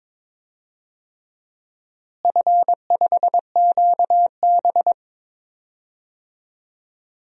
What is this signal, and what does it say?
Morse code, heard as a keyed tone, about 22 wpm, 700 Hz. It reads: F5QB
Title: DXCC-100 Call Signs - 4 Characters - Repeated 3x - 22wpm